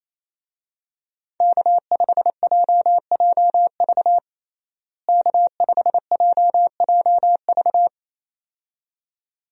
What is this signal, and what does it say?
Morse code, heard as a keyed tone, about 28 wpm, 700 Hz. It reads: K5JJV K5JJV
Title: USA Call Signs - 5 Characters - Repeated 2x - 28wpm